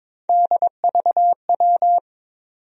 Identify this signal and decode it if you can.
Morse code, heard as a keyed tone, about 21 wpm, 700 Hz. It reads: DVW
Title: USA Call Sign Postfixes - 3 Letters 22wpm